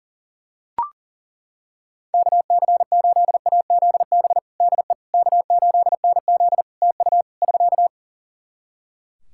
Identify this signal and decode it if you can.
Morse code, heard as a keyed tone, about 40 wpm, 700 Hz. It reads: KC8AZB DE K8NZ TU <SK>
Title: Practice QSO - Exchange 4 - Ending - 40wpm